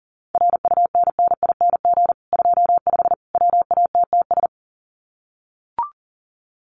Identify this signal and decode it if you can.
Morse code, heard as a keyed tone, about 40 wpm, 700 Hz. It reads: RUNNING 25 WATTS
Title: Common QSO Phrases 40wpm